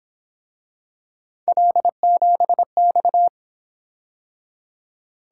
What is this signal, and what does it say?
Morse code, heard as a keyed tone, about 27 wpm, 700 Hz. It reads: L7X
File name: Speed-Racing - Three Letter-Number Combinations - 22wpm